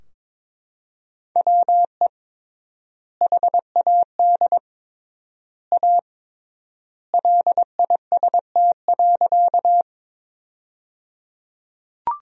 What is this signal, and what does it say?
Morse code, heard as a keyed tone, about 22 wpm, 700 Hz. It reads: WE HAD A LIST.
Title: 3x Word-Spacing - Sentences from Top 400 Words - 22wpm